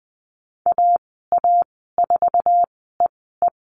Morse code, heard as a keyed tone, about 20 wpm, spaced as for 14 wpm, 700 Hz. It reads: AA4EE
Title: 048 Lesson 9g - Number 4 - Call Signs 20wpm